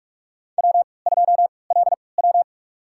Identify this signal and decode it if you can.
Morse code, heard as a keyed tone, about 45 wpm, spaced as for 25 wpm, 700 Hz. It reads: W2LW
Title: USA Call Signs - 4 Characters - 3x Character Spacing - 45wpm